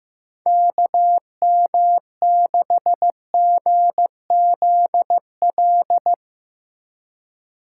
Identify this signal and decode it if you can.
Morse code, heard as a keyed tone, about 15 wpm, 700 Hz. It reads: KM6GZL
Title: USA Call Signs - 4x Character Spacing - 15wpm